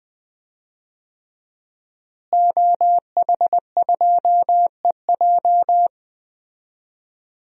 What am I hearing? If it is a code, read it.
Morse code, heard as a keyed tone, about 20 wpm, 700 Hz. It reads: OH2EJ
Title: DXCC-100 Call Signs - 5 Characters - Repeated 3x - 20wpm